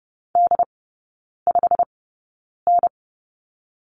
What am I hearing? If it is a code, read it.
Morse code, heard as a keyed tone, about 30 wpm, spaced as for 9 wpm, 700 Hz. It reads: D5N
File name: Lesson 16e - Letter c - Three Characters 30wpm